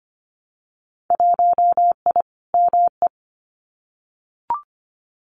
Morse code, heard as a keyed tone, about 25 wpm, 700 Hz. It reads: 1I ME
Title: Winter Field Day Exchanges Encore 25wpm